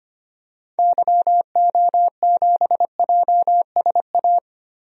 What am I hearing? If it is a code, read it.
Morse code, heard as a keyed tone, about 25 wpm, 700 Hz. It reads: YO7JSA
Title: DXCC-100 Call Signs - 6 Characters - 25wpm